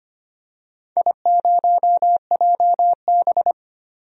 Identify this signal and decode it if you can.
Morse code, heard as a keyed tone, about 25 wpm, 700 Hz. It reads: I0JB
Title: DXCC-100 Call Signs - 4 Characters - 25wpm